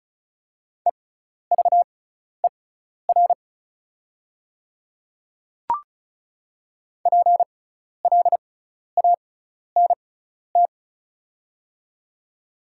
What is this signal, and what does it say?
Morse code, heard as a keyed tone, about 34 wpm, spaced as for 12 wpm, 690 Hz. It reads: EVER PLANT
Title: 6x Character Spacing - Top 500 Words - 35wpm